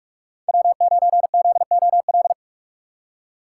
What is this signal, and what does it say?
Morse code, heard as a keyed tone, about 46 wpm, 700 Hz. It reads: W9ZOL
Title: USA Call Signs - 5 Characters - 4x Character Spacing - 45wpm